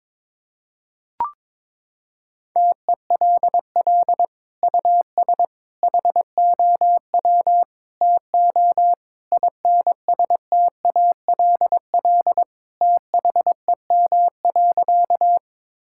Morse code, heard as a keyed tone, about 22 wpm, 700 Hz. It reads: TELL US HOW TO INSTALL THEM.
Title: Sentences from Top 1600 Words 22wpm